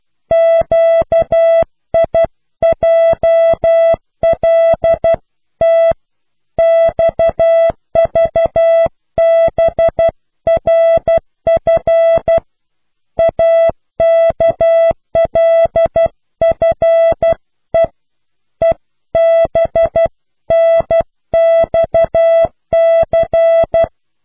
Morse code, heard as a keyed tone, about 12 wpm, 660 Hz. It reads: QIJLT XVBRF AKLFE EBNXC